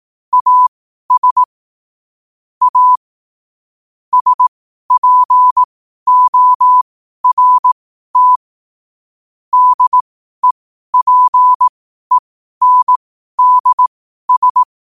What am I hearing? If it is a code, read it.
Morse code, heard as a keyed tone, about 18 wpm, spaced as for 12 wpm, 1000 Hz. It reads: AS A SPORT DEPENDS
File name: qotdfiles / 12wpm